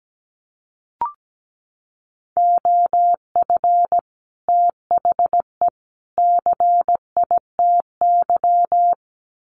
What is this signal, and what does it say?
Morse code, heard as a keyed tone, about 17 wpm, 700 Hz. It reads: OF THE CITY
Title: Sets of 3 Words 17wpm